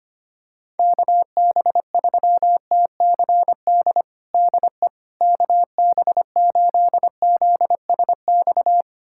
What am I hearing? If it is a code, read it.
Morse code, heard as a keyed tone, about 25 wpm, 700 Hz. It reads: KB3TCD DE KB8ZSX
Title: USA Callsign DE Callsign - 6 Characters - 25wpm